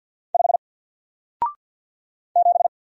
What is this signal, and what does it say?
Morse code, heard as a keyed tone, about 48 wpm, 700 Hz. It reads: 5 7
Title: Single Character (Letter:Number:Punctuation:Prosign) Alt 50wpm